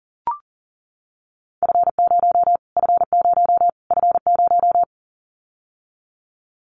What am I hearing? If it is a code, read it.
Morse code, heard as a keyed tone, about 40 wpm, 700 Hz. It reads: F0 F0 F0
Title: Callsign Prefixes for Top 25 DXCC - Repeated 3x - 40wpm